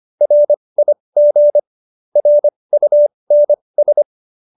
Morse code, heard as a keyed tone, about 25 wpm, spaced as for 20 wpm, 590 Hz. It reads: RIG RUNS
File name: QSO309 20